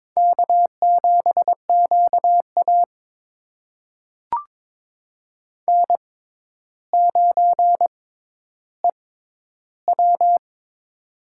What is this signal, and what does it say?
Morse code, heard as a keyed tone, about 22 wpm, spaced as for 11 wpm, 700 Hz. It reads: K7QA N9EW
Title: USA Call Signs - 4 Characters - 6x Character Spacing - 22wpm